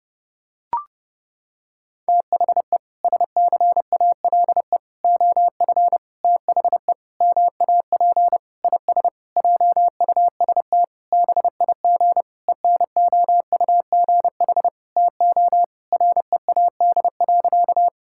Morse code, heard as a keyed tone, about 32 wpm, 700 Hz. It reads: THE SCALE OF THE MAP IS JUST BIG ENOUGH TO READ.
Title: Sentences from Top 1000 Words - Review Part 5 - 30wpm